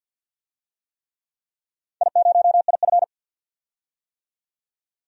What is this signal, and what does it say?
Morse code, heard as a keyed tone, about 45 wpm, 700 Hz. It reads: I0IF
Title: DXCC-100 Call Signs - 4 Characters - 50wpm